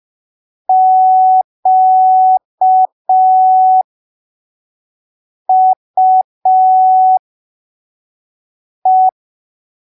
Morse code, heard as a keyed tone, about 5 wpm, spaced as for 3 wpm, 750 Hz. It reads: QUE
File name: arquivos / texto / texto 07 05ppm